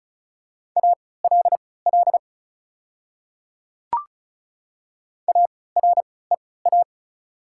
Morse code, heard as a keyed tone, about 34 wpm, spaced as for 20 wpm, 700 Hz. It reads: ALL AREA
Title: 3x Character Spacing - Top 500 Words - 35wpm